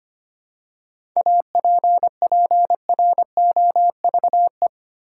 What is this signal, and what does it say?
Morse code, heard as a keyed tone, about 25 wpm, 700 Hz. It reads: APPROVE
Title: Seven Letter Words - 25wpm